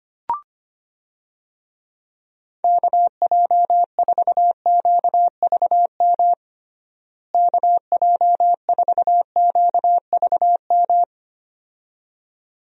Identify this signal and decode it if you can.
Morse code, heard as a keyed tone, about 25 wpm, 700 Hz. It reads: KJ4QVM KJ4QVM
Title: USA Call Signs - 6 Characters - Repeated 2x - 25wpm